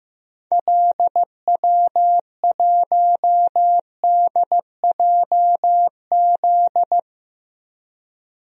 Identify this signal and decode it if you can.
Morse code, heard as a keyed tone, about 15 wpm, 700 Hz. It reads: LW1DJZ
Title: DXCC-100 Call Signs - 6 Characters - Repeated 2x - 15wpm